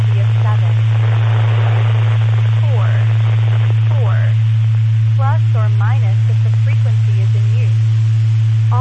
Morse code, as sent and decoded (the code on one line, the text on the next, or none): none